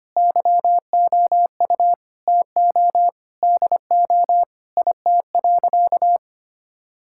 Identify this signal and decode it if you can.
Morse code, heard as a keyed tone, about 25 wpm, 700 Hz. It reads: YOU TO DO IT.
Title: Sentences from Top 1000 Words - Review Part 5 - 25wpm